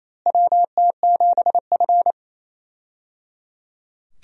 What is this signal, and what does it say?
Morse code, heard as a keyed tone, about 28 wpm, 700 Hz. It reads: WT7F
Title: USA Call Signs - 4 Characters - Repeated 3x - 28wpm